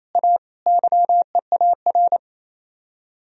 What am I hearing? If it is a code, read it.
Morse code, heard as a keyed tone, about 36 wpm, 700 Hz. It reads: A YEAR